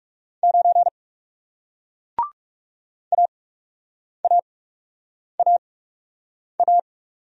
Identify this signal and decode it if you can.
Morse code, heard as a keyed tone, about 40 wpm, spaced as for 8 wpm, 700 Hz. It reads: 9 AAAA